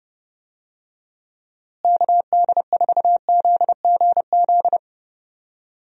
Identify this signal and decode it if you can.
Morse code, heard as a keyed tone, about 30 wpm, 700 Hz. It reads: KD4ZGZ
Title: USA Call Signs - 6 Characters - Repeated 3x - 30wpm